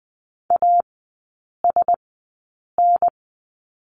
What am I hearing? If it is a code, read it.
Morse code, heard as a keyed tone, about 20 wpm, spaced as for 8 wpm, 700 Hz. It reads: ASN
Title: 035 Lesson 7e - Letter s - Three Characters 20wpm